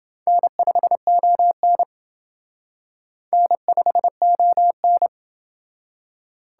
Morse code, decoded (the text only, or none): N5ON N5ON